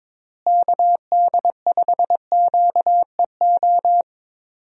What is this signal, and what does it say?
Morse code, heard as a keyed tone, about 22 wpm, 700 Hz. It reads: KD5QEO
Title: USA Call Signs - 6 Characters - Repeated 3x - 22wpm